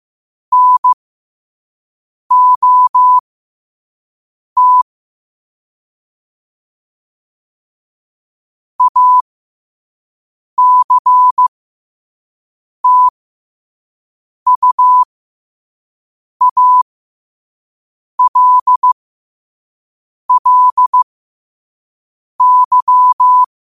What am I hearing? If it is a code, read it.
Morse code, heard as a keyed tone, about 15 wpm, spaced as for 5 wpm, 1000 Hz. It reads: NOT ACTUALLY
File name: qotdfiles / 5wpm